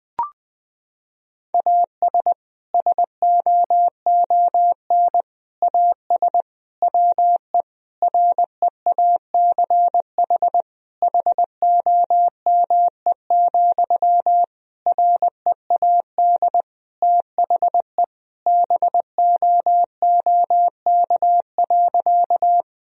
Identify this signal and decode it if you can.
Morse code, heard as a keyed tone, about 21 wpm, 700 Hz. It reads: AS SOON AS WE REACH HOME, READ THE BOOK.
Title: Sentences from Top 1000 Words - Review Part 2 - 20wpm